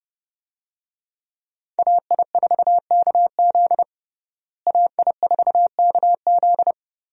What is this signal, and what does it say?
Morse code, heard as a keyed tone, about 32 wpm, 700 Hz. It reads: AI4KZ AI4KZ
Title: USA Call Signs - 5 Characters - Repeated 3x - 30wpm